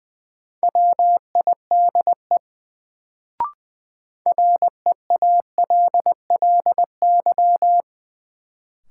Morse code, heard as a keyed tone, about 20 wpm, 700 Hz. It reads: WIDE REALLY
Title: Sets of 4 Words - New Word Drill - 20wpm